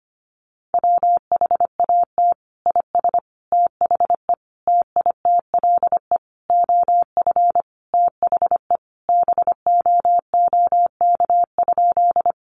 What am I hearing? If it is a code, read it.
Morse code, heard as a keyed tone, about 25 wpm, 700 Hz. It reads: WHAT IS THE TITLE OF THE BOOK?
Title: Sentences from Top 1200 Words 25wpm